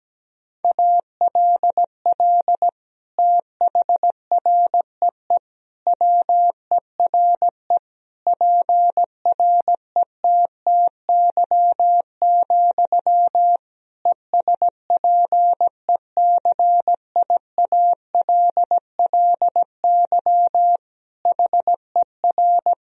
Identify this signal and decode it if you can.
Morse code, heard as a keyed tone, about 17 wpm, 700 Hz. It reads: ALL THREE WERE PRETTY, ESPECIALLY HER
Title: Sentences from Top 1000 Words - Review Part 4 - 17wpm